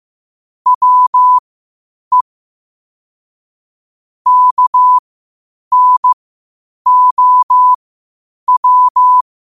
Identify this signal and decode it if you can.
Morse code, heard as a keyed tone, about 15 wpm, spaced as for 8 wpm, 1000 Hz. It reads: WE KNOW
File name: qotdfiles / 8wpm